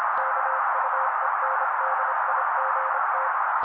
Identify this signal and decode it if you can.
Morse code, heard as a keyed tone, about 26 wpm, 530 Hz. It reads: KUENDIGT
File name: daily / 25wpm noise